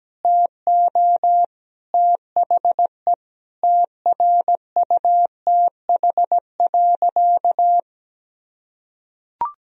Morse code, heard as a keyed tone, about 17 wpm, 700 Hz. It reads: TO THE TRUTH.
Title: Sentences from Top 1000 Words - Review Part 4 - 17wpm